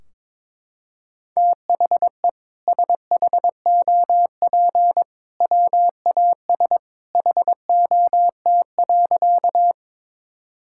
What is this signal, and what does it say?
Morse code, heard as a keyed tone, about 22 wpm, 700 Hz. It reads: THE SHOP WAS HOT.
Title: Sentences from Top 1600 Words 22wpm